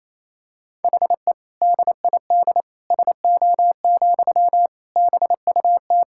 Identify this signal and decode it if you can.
Morse code, heard as a keyed tone, about 28 wpm, 700 Hz. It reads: HE DID SO, BUT
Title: Sentences from Top 1000 Words - Review Part 5 - 28wpm